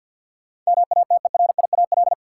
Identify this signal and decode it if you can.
Morse code, heard as a keyed tone, about 51 wpm, 700 Hz. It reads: MATERIAL
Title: Sentences from Top 1100 Words - New Word Drill Alt - 50wpm